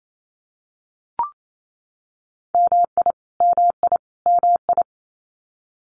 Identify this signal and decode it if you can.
Morse code, heard as a keyed tone, about 28 wpm, 700 Hz. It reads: MI MI MI
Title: US States Repeated 3x 28wpm